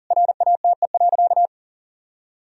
Morse code, heard as a keyed tone, about 37 wpm, 700 Hz. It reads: RATE.
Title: Sentences from Top 1000 Words - Review Part 4 - 40wpm